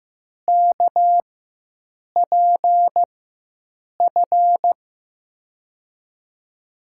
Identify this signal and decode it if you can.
Morse code, heard as a keyed tone, about 16 wpm, spaced as for 7 wpm, 700 Hz. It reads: KPF
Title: USA Call Signs - 6 Characters - 4x Character Spacing - 15wpm